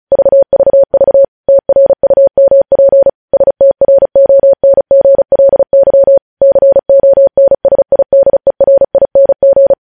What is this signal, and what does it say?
Morse code, heard as a keyed tone, about 36 wpm, 550 Hz. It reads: VVV TRUMP STRONGLY CONSIDERING
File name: podcast / News headlines-1x-35x35